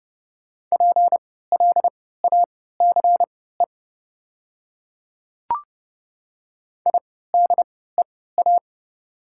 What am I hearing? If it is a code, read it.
Morse code, heard as a keyed tone, about 30 wpm, spaced as for 17 wpm, 690 Hz. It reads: PLACE IDEA
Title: 3x Character Spacing - Top 500 Words - 30wpm